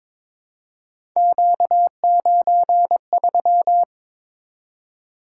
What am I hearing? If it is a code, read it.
Morse code, heard as a keyed tone, about 23 wpm, 700 Hz. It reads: Q93
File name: practice / 22wpm / Three Letter-Number Combinations - 22wpm